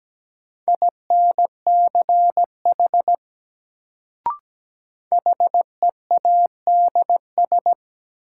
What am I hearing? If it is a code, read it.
Morse code, heard as a keyed tone, about 17 wpm, 700 Hz. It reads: INCH HEADS
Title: Sentences from Top 1000 Words - Review Part 3 - New Word Drill - 17wpm